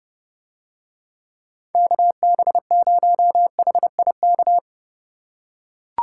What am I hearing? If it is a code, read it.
Morse code, heard as a keyed tone, about 30 wpm, 700 Hz. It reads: KB0HIK